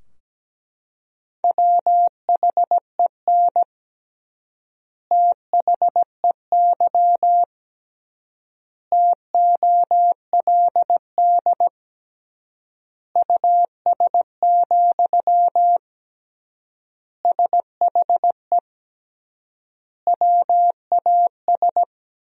Morse code, heard as a keyed tone, about 17 wpm, 700 Hz. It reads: WHEN THEY TOLD US, SHE WAS
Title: 3x Word-Spacing - Sentences from Top 400 Words - 17wpm